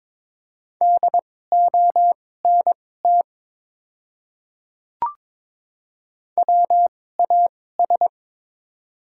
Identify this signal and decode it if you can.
Morse code, heard as a keyed tone, about 22 wpm, spaced as for 16 wpm, 700 Hz. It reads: DONT WAS
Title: 2x Character Spacing - Top 300 Words - 22wpm